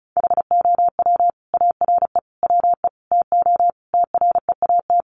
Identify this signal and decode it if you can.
Morse code, heard as a keyed tone, about 34 wpm, 700 Hz. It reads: HOW ARE WE TO TREAT
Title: Sentences from Top 1300 Words 35wpm